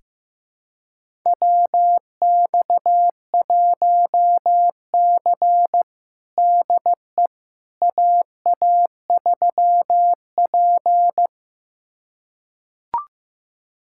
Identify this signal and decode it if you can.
Morse code, heard as a keyed tone, about 16 wpm, 700 Hz. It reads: WX1C DE AA3P
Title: USA Callsign DE Callsign - 4 Characters - 15wpm